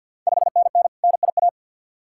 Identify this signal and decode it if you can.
Morse code, heard as a keyed tone, about 51 wpm, 690 Hz. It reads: 5NN NIA
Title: Ontario QSO Party Exchange 50wpm